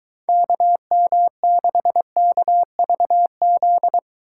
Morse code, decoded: KM6KVZ

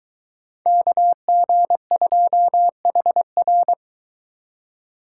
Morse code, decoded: KG2HR

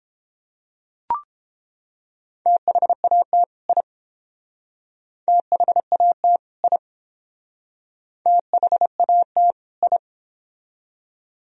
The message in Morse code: THAT I THAT I THAT I

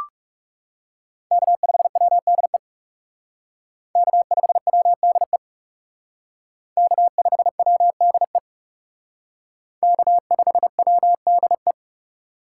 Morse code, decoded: K5WDE K5WDE K5WDE K5WDE

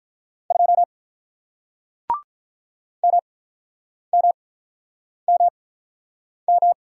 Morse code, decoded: 2 M M M M

